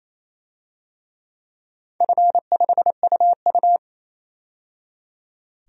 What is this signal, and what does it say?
Morse code, heard as a keyed tone, about 28 wpm, 700 Hz. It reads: F5UU
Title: DXCC-100 Call Signs - 4 Characters - 28wpm